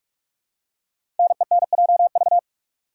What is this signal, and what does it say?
Morse code, heard as a keyed tone, about 44 wpm, 700 Hz. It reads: NENJV
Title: Code Groups - 5 Letters - 45wpm